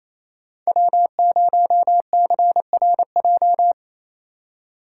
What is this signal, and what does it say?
Morse code, heard as a keyed tone, about 28 wpm, 700 Hz. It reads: W0CRJ
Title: USA Call Signs - 4x Character Spacing - 28wpm